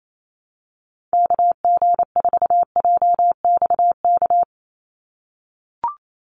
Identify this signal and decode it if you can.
Morse code, heard as a keyed tone, about 28 wpm, 700 Hz. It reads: KG4JXK
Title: USA Call Signs 28wpm